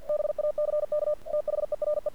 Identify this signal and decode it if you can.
Morse code, heard as a keyed tone, about 47 wpm, 600 Hz. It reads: BACK THERE
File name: archived / CRSnet